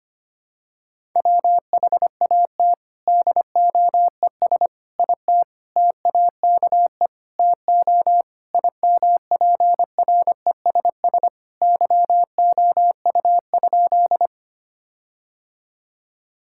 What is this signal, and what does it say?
Morse code, heard as a keyed tone, about 25 wpm, 700 Hz. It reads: WHAT DOES IT TAKE TO IMPRESS YOU?
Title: Sentences from Top 1600 Words 25wpm